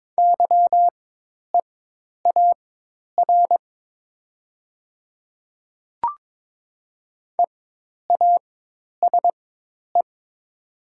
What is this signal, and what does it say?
Morse code, heard as a keyed tone, about 22 wpm, spaced as for 10 wpm, 700 Hz. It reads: YEAR EASE
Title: 4x Character Spacing - Top 300 Words - 22wpm